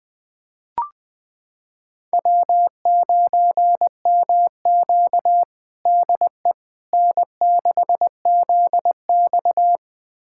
Morse code, heard as a keyed tone, about 20 wpm, 700 Hz. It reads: W9MQ DE N6ZX
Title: USA Callsign DE Callsign - 4 Characters - 20wpm